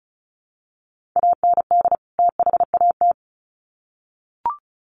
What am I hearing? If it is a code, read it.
Morse code, heard as a keyed tone, about 34 wpm, 700 Hz. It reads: AND THAT
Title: Sets of 2 Words 35wpm